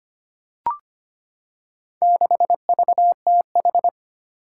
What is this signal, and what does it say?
Morse code, heard as a keyed tone, about 25 wpm, 700 Hz. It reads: 6VTH
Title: Four Challenging Letter-Number Combinations 25wpm